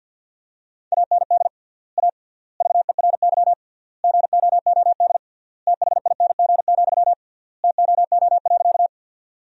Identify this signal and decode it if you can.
Morse code, heard as a keyed tone, about 47 wpm, 700 Hz. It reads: AND A VERY GOOD THING, TOO.
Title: 3x Word-Spacing - Sentences from Top 200 Words - 50wpm